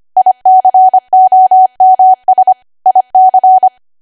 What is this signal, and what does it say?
Morse code, heard as a keyed tone, about 25 wpm, 750 Hz. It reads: ICOMS IC
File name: Archive / 25 WPM / 25WPM